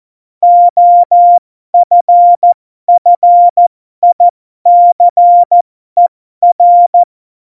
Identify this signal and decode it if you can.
Morse code, heard as a keyed tone, about 14 wpm, spaced as for 12 wpm, 700 Hz. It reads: OFFICER
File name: Morsekurs / Witze